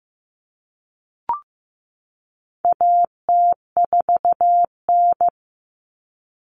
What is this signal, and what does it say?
Morse code, heard as a keyed tone, about 15 wpm, 700 Hz. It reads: AT4N
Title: Common Cut Numbers - 4-Digit 15wpm